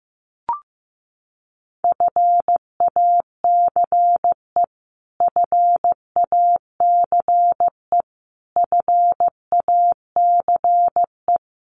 Morse code, heard as a keyed tone, about 15 wpm, 700 Hz. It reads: FACE FACE FACE
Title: Top 500 Words Repeated 15wpm